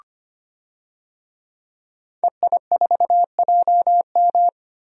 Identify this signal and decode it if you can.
Morse code, heard as a keyed tone, about 25 wpm, 700 Hz. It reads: EI4JM
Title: DXCC-100 Call Signs - 5 Characters - 25wpm